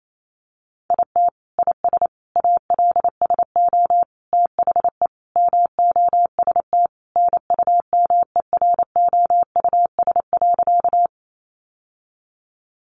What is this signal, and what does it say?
Morse code, heard as a keyed tone, about 28 wpm, 700 Hz. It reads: IT IS ALSO THE MOST NUMEROUS.
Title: Sentences from Top 1400 Words 28wpm